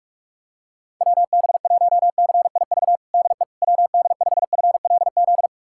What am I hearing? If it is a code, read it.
Morse code, heard as a keyed tone, about 41 wpm, 690 Hz. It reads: WB1CIV DE WD5FLZ
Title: USA Callsign DE Callsign - 6 Characters - 45wpm